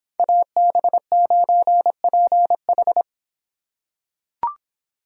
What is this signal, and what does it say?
Morse code, heard as a keyed tone, about 26 wpm, 700 Hz. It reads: AB9PH